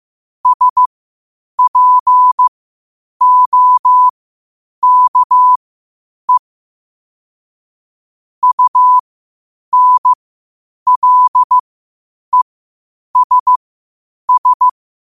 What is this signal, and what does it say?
Morse code, heard as a keyed tone, about 15 wpm, spaced as for 8 wpm, 1000 Hz. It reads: SPOKE UNLESS
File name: qotdfiles / 8wpm